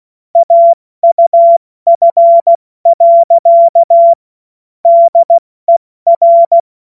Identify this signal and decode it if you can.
Morse code, heard as a keyed tone, about 16 wpm, 680 Hz. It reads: AUF. DER
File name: Morsekurs / Witze / Witz 07. Dezember 2021, 70 F80